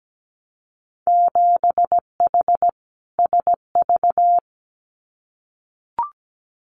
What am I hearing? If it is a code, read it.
Morse code, heard as a keyed tone, about 17 wpm, 700 Hz. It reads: 7H SV
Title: Winter Field Day Exchanges Encore 17wpm